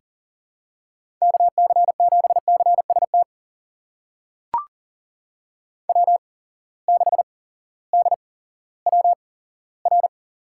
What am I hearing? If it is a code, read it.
Morse code, heard as a keyed tone, about 40 wpm, spaced as for 18 wpm, 700 Hz. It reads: KC7CST W6DWR
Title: USA Call Signs - 8x Character Spacing - 40wpm